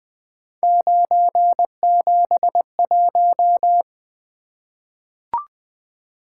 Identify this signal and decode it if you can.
Morse code, heard as a keyed tone, about 20 wpm, 700 Hz. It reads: 971
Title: Three Letter-Number Combinations - 20wpm